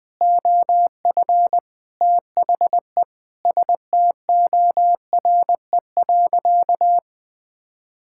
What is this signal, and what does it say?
Morse code, heard as a keyed tone, about 20 wpm, 700 Hz. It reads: OF THE STORE.
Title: Sentences from Top 1000 Words - Review Part 5 - 20wpm